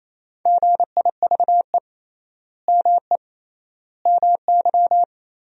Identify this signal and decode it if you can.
Morse code, heard as a keyed tone, about 28 wpm, 700 Hz. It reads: GIVE ME MY
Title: 3x Word-Spacing - Sentences from Top 200 Words - 28wpm